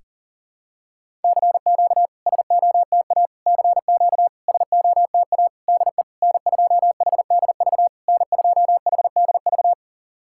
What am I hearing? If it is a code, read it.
Morse code, heard as a keyed tone, about 40 wpm, 690 Hz. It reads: CQ SOTA CQ SOTA DE N2HDV N2HDV
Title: SOTA QSO Practice 40wpm